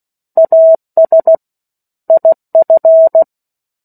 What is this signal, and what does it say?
Morse code, heard as a keyed tone, about 16 wpm, 650 Hz. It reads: AS IF
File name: 03-Chapter-03 16 WPM